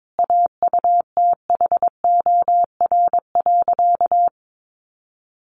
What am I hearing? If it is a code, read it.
Morse code, heard as a keyed tone, about 22 wpm, 700 Hz. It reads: AUTHOR.
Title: Sentences from Top 1200 Words 22wpm